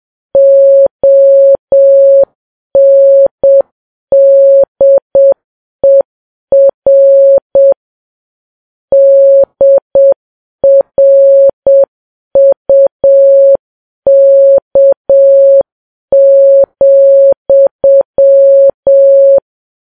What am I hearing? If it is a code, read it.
Morse code, heard as a keyed tone, about 7 wpm, 550 Hz. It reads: ONDER DRUK,